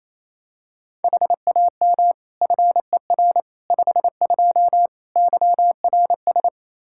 Morse code, heard as a keyed tone, about 28 wpm, 700 Hz. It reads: HAM FER 52 YRS
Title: Practice QSO - Exchange 3 - More Info - 28wpm